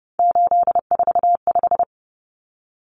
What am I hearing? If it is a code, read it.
Morse code, heard as a keyed tone, about 30 wpm, 700 Hz. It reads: 845
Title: Common Cut Numbers - 3-Digit 30wpm